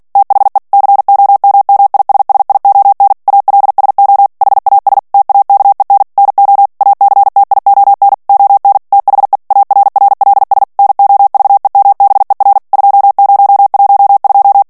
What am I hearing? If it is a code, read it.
Morse code, heard as a keyed tone, about 48 wpm, 800 Hz. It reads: THE COMMISSION ALSO HAS TAKEN NO ACTION ON THE ARRLS NOVEMBER 2012
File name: ANGLAIS texte clair 3 vitesse 45 mots minute